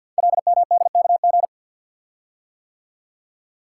Morse code, heard as a keyed tone, about 50 wpm, 700 Hz. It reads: LMDKG
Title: Code Groups - 5 Letters - 50wpm